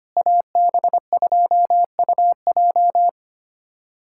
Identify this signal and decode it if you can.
Morse code, heard as a keyed tone, about 25 wpm, 700 Hz. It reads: AB2UJ
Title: USA Callsign DE Callsign - 5 Characters - 25wpm